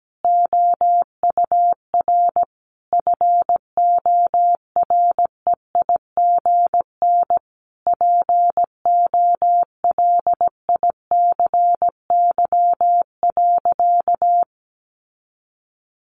Morse code, heard as a keyed tone, about 17 wpm, 700 Hz. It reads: OUR FOREIGN POLICY.
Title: Sentences from Top 1400 Words 17wpm